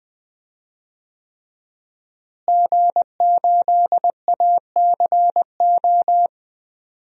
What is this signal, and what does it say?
Morse code, heard as a keyed tone, about 20 wpm, 700 Hz. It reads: G8ACO
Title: DXCC-100 Call Signs - 5 Characters - Repeated 3x - 20wpm